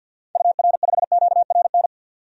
Morse code, heard as a keyed tone, about 50 wpm, 700 Hz. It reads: UR5QRN
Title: DXCC-100 Call Signs - 6 Characters - 50wpm